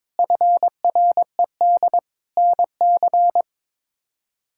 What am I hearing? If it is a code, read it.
Morse code, heard as a keyed tone, about 22 wpm, 700 Hz. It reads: FRED NC
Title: K1USN SST QSO Practice - 22wpm